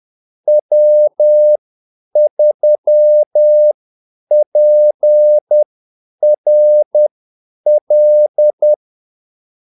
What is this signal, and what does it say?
Morse code, heard as a keyed tone, about 10 wpm, spaced as for 8 wpm, 600 Hz. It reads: W3PRL
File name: Callsigns-10-8wpm